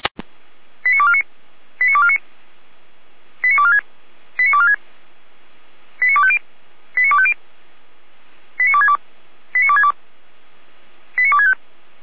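Five-tone selective call (ZVEI), heard as quick runs of five tones, tones of about 70 ms. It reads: 89249 89249 89246 89246 89250 89250 89262 89262 89265
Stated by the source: ZVEI five-tone call received on SDR.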